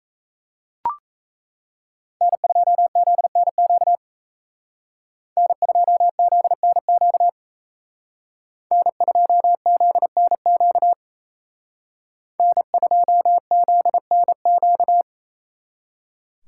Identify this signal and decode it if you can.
Morse code, heard as a keyed tone, about 36 wpm, 700 Hz. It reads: N2ZNQ N2ZNQ N2ZNQ N2ZNQ